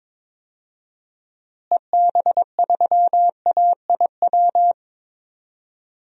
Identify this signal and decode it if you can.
Morse code, heard as a keyed tone, about 21 wpm, 700 Hz. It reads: EB3AIW
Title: DXCC-100 Call Signs - 6 Characters - Repeated 3x - 22wpm